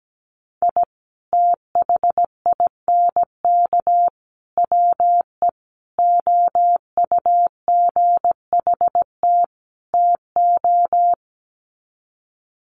Morse code, heard as a keyed tone, about 17 wpm, 700 Hz. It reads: I THINK WE OUGHT TO
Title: Sets of 5 Words Encore 17wpm